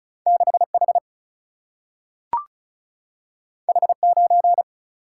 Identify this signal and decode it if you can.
Morse code, heard as a keyed tone, about 36 wpm, 700 Hz. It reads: 6 H H 9
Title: Single-Letter Single-Number 35wpm